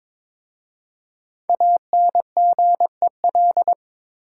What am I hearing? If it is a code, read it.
Morse code, heard as a keyed tone, about 22 wpm, 700 Hz. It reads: ANGEL
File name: 5 Letter Words - First 3 Letters Given - 22wpm